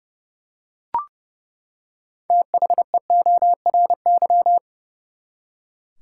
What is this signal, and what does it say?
Morse code, heard as a keyed tone, about 30 wpm, 700 Hz. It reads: THEORY
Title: Six Letter Words - 30wpm